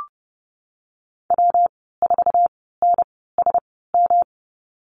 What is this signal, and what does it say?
Morse code, heard as a keyed tone, about 31 wpm, 700 Hz. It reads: W 4 N S M
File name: Lesson 19g - Letter w - Call Signs 30wpm